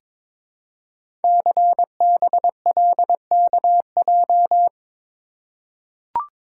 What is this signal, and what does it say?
Morse code, heard as a keyed tone, about 22 wpm, 700 Hz. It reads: CBLKJ
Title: Code Groups - 5 Letters - 22wpm